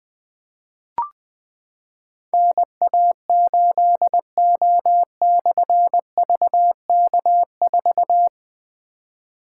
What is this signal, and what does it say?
Morse code, heard as a keyed tone, about 20 wpm, 700 Hz. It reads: NA8O/VK4
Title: Difficult Call Signs 20wpm